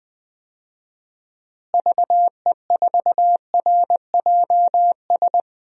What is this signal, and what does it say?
Morse code, heard as a keyed tone, about 20 wpm, 700 Hz. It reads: VE4RJS
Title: DXCC-100 Call Signs - 6 Characters - Repeated 3x - 20wpm